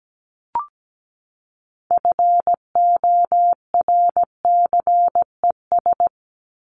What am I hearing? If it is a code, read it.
Morse code, heard as a keyed tone, about 17 wpm, 700 Hz. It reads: FORCES
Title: ES:S Suffix - Top 100 Words with Base Words 17wpm